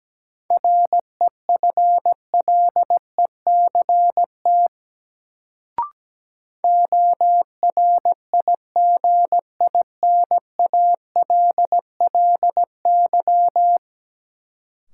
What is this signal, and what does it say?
Morse code, heard as a keyed tone, about 17 wpm, 700 Hz. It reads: REFLECT ORIGINALLY
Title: Sentences from Top 1500 Words - New Word Drill Alt - 17wpm